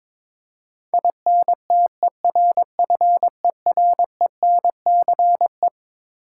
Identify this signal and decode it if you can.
Morse code, heard as a keyed tone, about 22 wpm, 700 Hz. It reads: INTERFERENCE
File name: Twelve Letter Words - 22wpm